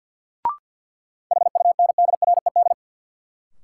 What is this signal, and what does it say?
Morse code, heard as a keyed tone, about 49 wpm, 700 Hz. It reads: HUNDRED
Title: Sentences from Top 400 Words - New Word Drill - 50wpm